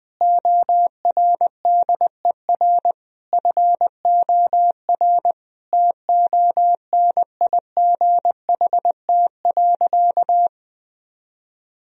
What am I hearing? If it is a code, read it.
Morse code, heard as a keyed tone, about 20 wpm, 700 Hz. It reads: ORDER FOR TONIGHT.
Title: Sentences from Top 1000 Words - Review Part 3 - 20wpm